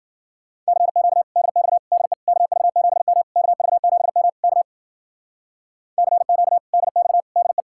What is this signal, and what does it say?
Morse code, heard as a keyed tone, about 51 wpm, 700 Hz. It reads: CQ DX DE KV7M KV7M K CQ DX DE